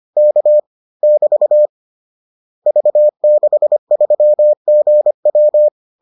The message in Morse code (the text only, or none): K <BT> V63GW